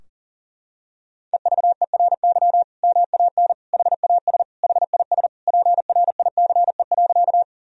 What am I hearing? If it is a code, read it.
Morse code, heard as a keyed tone, about 40 wpm, 700 Hz. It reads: EVERY MAN HAS HIS PRICE.
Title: Sentences from Top 1000 Words - Review Part 4 - 40wpm